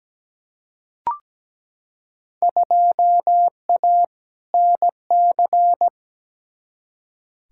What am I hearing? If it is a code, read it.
Morse code, heard as a keyed tone, about 17 wpm, 700 Hz. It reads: 2A NC
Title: Field Day Exchanges - 17wpm